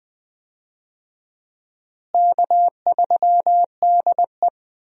Morse code, heard as a keyed tone, about 20 wpm, 700 Hz. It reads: K3DE
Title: USA Call Signs - 4 Characters - Repeated 2x - 20wpm